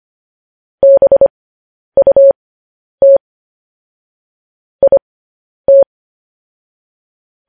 Morse code, heard as a keyed tone, about 25 wpm, spaced as for 10 wpm, 550 Hz. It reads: BUT IT